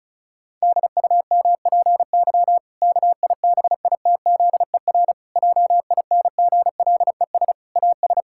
Morse code, heard as a keyed tone, about 36 wpm, 700 Hz. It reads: DUMPY KIBITZER JINGLES AS
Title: Pangram Challenge v2 - 35wpm